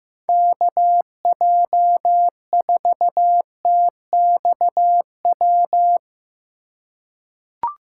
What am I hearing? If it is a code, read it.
Morse code, heard as a keyed tone, about 15 wpm, 700 Hz. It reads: KJ4TXW
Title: USA Call Signs - 6 Characters - 2x Character Spacing - 15wpm